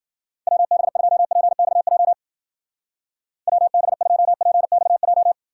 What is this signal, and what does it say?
Morse code, heard as a keyed tone, about 49 wpm, 700 Hz. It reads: WB2PXJ WB2PXJ
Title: Speed Racing - USA 2x3 Call Signs 40wpm